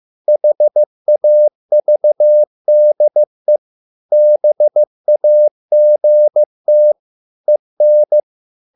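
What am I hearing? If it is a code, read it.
Morse code, heard as a keyed tone, about 15 wpm, 600 Hz. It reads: HAVDE BAGT EN
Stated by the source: Morse code at a slow 15 wpm from CW med Gnister.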